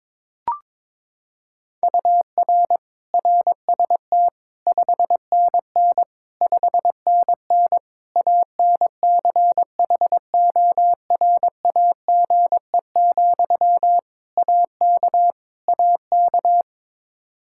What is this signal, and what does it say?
Morse code, heard as a keyed tone, about 22 wpm, 700 Hz. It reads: UR RST 5NN 5NN ANCHORAGE, AK AK
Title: Practice QSO - Exchange 1 - Beginning - 22wpm